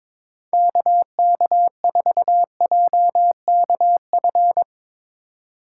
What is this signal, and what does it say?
Morse code, heard as a keyed tone, about 22 wpm, 700 Hz. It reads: KK4JKF
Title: USA Call Signs - 6 Characters - 4x Character Spacing - 22wpm